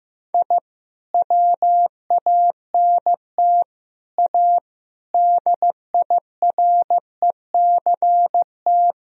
Morse code, heard as a keyed tone, about 15 wpm, 700 Hz. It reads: I WANT A DIRECT
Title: Sentences from Top 1000 Words - Review Part 2 - 15wpm